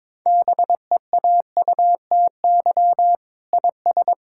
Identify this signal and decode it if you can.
Morse code, heard as a keyed tone, about 22 wpm, 700 Hz. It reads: BEAUTY IS
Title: Sentences from Top 1000 Words - Review Part 5 - 22wpm